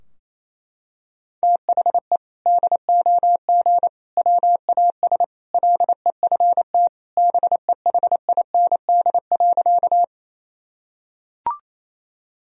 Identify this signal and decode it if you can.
Morse code, heard as a keyed tone, about 28 wpm, 700 Hz. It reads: THE DOG WAS LEFT BEHIND.
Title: Sentences from Top 1000 Words - Review Part 2 - 28wpm